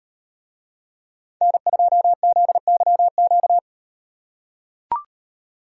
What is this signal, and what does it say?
Morse code, heard as a keyed tone, about 38 wpm, 700 Hz. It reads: N2ZYQ